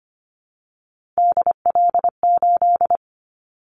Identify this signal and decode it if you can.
Morse code, heard as a keyed tone, about 25 wpm, 700 Hz. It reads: DL8
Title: Callsign Prefixes for Top 50 DXCC - 25wpm